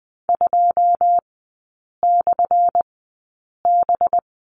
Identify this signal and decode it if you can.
Morse code, heard as a keyed tone, about 20 wpm, spaced as for 8 wpm, 690 Hz. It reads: 2/B
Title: 191 Lesson 31e - Letter v - Three Characters 20wpm